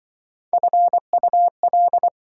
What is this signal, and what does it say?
Morse code, heard as a keyed tone, about 25 wpm, 700 Hz. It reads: FUL